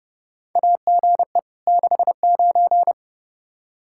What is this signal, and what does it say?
Morse code, heard as a keyed tone, about 31 wpm, 700 Hz. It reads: AGE 69
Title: Practice QSO - Exchange 3 - More Info - 30wpm